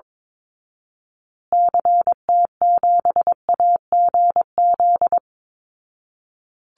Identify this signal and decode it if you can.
Morse code, heard as a keyed tone, about 22 wpm, 700 Hz. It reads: CT7AGZ
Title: Call Signs Repeated 2x - 22wpm